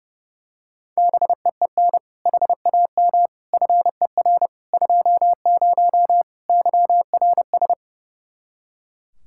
Morse code, heard as a keyed tone, about 32 wpm, 700 Hz. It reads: BEEN HAM FER 20 YRS
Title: K1USN SST QSO Practice 30wpm